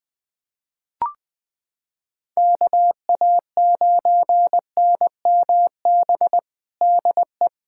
Morse code, heard as a keyed tone, about 20 wpm, 700 Hz. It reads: KA9NMB DE
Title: USA Callsign DE Callsign 20wpm